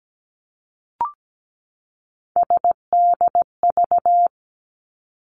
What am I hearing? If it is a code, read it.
Morse code, heard as a keyed tone, about 17 wpm, 700 Hz. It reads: SDV
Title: Three Challenging Letter-Number Combinations 17wpm